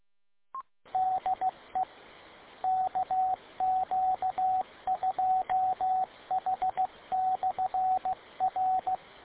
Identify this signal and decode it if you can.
Morse code, heard as a keyed tone, about 16 wpm, 750 Hz, with radio noise. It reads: DE KQ2H/R